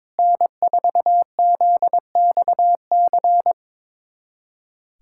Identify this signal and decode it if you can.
Morse code, heard as a keyed tone, about 22 wpm, 700 Hz. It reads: N4ZXC
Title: USA Callsign DE Callsign - 5 Characters - 22wpm